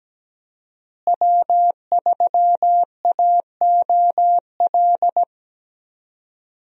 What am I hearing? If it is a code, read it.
Morse code, heard as a keyed tone, about 17 wpm, 700 Hz. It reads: W3AOL
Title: USA Call Signs - 5 Characters - Repeated 2x - 17wpm